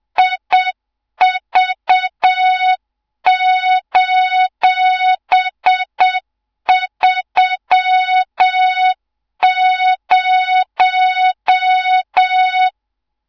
Morse code, heard as a keyed tone, about 7 wpm, 3700 Hz, with noise. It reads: IV:30
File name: Morze-HA2MN / hang / 24 GYAKORLAT